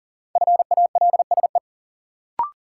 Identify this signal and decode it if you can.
Morse code, heard as a keyed tone, about 40 wpm, 700 Hz. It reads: FALSE
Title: Sentences from Top 1600 Words - New Word Drill - 40wpm